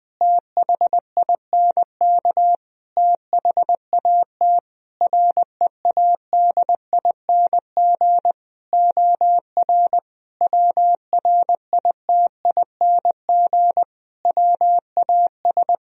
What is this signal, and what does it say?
Morse code, heard as a keyed tone, about 20 wpm, 700 Hz. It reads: THINK THAT READING OR WRITING WAS
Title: Sentences from Top 1000 Words - Review Part 3 - 20wpm